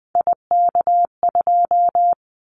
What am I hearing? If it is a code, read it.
Morse code, heard as a keyed tone, about 20 wpm, 700 Hz. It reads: IK2
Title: Callsign Prefixes for Top 25 DXCC - Repeated 3x - 20wpm